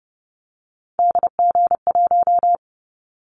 Morse code, heard as a keyed tone, about 30 wpm, 700 Hz. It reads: DG1
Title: Callsign Prefixes for Top 25 DXCC - 30wpm